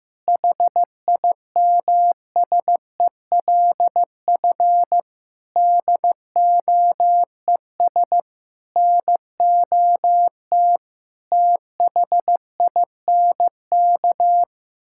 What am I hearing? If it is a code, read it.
Morse code, heard as a keyed tone, about 15 wpm, 700 Hz. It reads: HIMSELF DOES NOT THINK
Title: Sentences from Top 1600 Words 15wpm